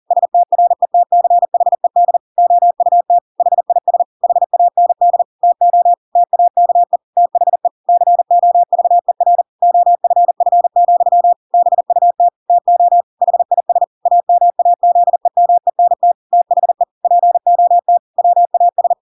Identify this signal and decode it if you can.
Morse code, heard as a keyed tone, about 40 wpm, 700 Hz. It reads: STRETCHED OUT HIS HAND TO TAKE THE COVER OFF, BUT TO HIS AMAZEMENT THE POT WAS
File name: Pinocchio / Pinocchio ch5 40 40 ik1hge